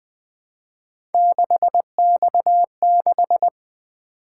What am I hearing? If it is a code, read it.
Morse code, heard as a keyed tone, about 20 wpm, 700 Hz. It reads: 6X6
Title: Three Letter-Number Combinations - 20wpm